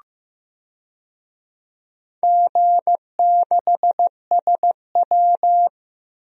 Morse code, heard as a keyed tone, about 15 wpm, 700 Hz. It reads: G6SW
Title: DXCC-100 Call Signs - 4 Characters - Repeated 2x - 15wpm